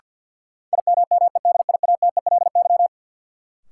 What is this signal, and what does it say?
Morse code, heard as a keyed tone, about 53 wpm, 700 Hz. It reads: IMMEDIATELY
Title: Sentences from Top 1100 Words - New Word Drill - 50wpm